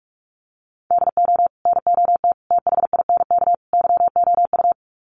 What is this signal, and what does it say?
Morse code, heard as a keyed tone, about 44 wpm, 700 Hz. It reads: DO NOT THINK YOU
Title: Sets of 4 Words 45wpm